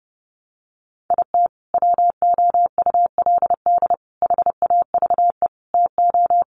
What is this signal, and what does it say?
Morse code, heard as a keyed tone, about 30 wpm, 700 Hz. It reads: IT WOULD HAVE TO
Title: Sets of 5 Words Encore 30wpm